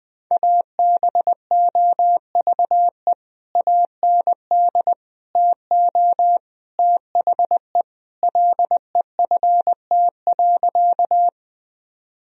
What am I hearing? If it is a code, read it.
Morse code, heard as a keyed tone, about 20 wpm, 700 Hz. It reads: ABOVE AND TO THE LEFT.
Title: Sentences from Top 1000 Words - Review Part 2 - 20wpm